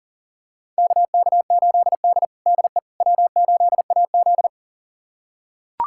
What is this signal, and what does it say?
Morse code, heard as a keyed tone, about 40 wpm, 700 Hz. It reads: KK8D DE W8AZ
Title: USA Callsign DE Callsign - 4 Characters - 40wpm